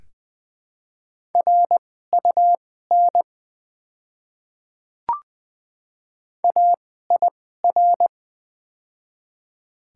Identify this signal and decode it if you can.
Morse code, heard as a keyed tone, about 20 wpm, spaced as for 14 wpm, 700 Hz. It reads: RUN AIR
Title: 2x Character Spacing - Top 300 Words - 20wpm